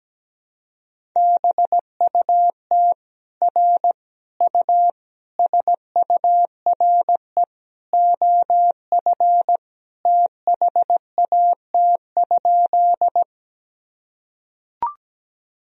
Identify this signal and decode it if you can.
Morse code, heard as a keyed tone, about 17 wpm, 700 Hz. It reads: BUT R U SURE OF THAT?
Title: Sentences from Top 300 Words and Abbreviations 17wpm